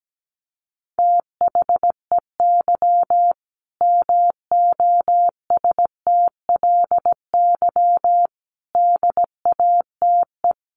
Morse code, heard as a keyed tone, about 17 wpm, 700 Hz. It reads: THEY MOSTLY DATE
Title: Sentences from Top 1300 Words 17wpm